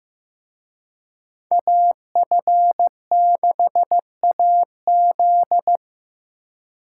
Morse code, heard as a keyed tone, about 18 wpm, 700 Hz. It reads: AF6AZ
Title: Speed Racing - USA 2x2 Call Signs 15wpm